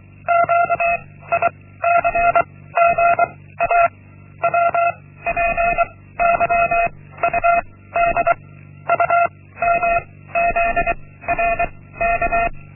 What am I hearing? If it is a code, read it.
Morse code, heard as a keyed tone, about 24 wpm, 690 Hz. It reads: Q I C G A W P Y U D U M Z R K
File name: Linksmakalnis-1